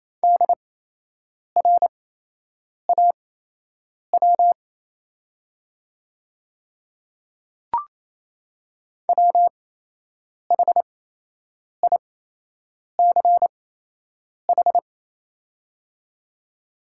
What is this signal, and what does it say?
Morse code, heard as a keyed tone, about 28 wpm, spaced as for 8 wpm, 700 Hz. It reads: DRAW WHICH
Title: 8x Character Spacing - Top 300 Words - 28wpm